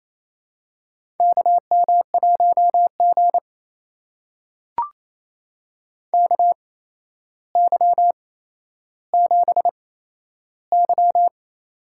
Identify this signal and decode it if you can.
Morse code, heard as a keyed tone, about 28 wpm, spaced as for 12 wpm, 700 Hz. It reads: KM1G KY7Y
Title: USA Call Signs - 4 Characters - 8x Character Spacing - 28wpm